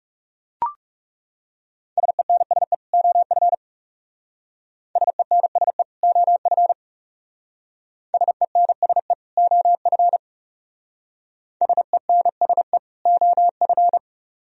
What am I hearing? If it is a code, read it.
Morse code, heard as a keyed tone, about 37 wpm, 700 Hz. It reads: SENSE OF SENSE OF SENSE OF SENSE OF
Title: Speed-Racing - Sets of 2 Words 30wpm